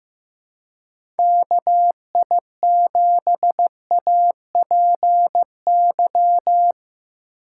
Morse code, heard as a keyed tone, about 15 wpm, 700 Hz. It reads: KI7APY
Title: USA Call Signs - 2x Character Spacing - 15wpm